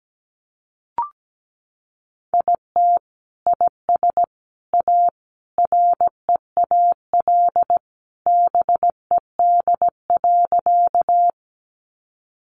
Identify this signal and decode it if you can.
Morse code, heard as a keyed tone, about 17 wpm, 700 Hz. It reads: IT IS A REAL BED.
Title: Sentences from Top 500 Words 17wpm